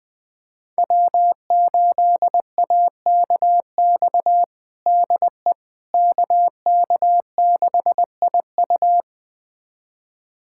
Morse code, heard as a keyed tone, about 21 wpm, 700 Hz. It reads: W8AKX DE KK6IU
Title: USA Callsign DE Callsign - 5 Characters - 20wpm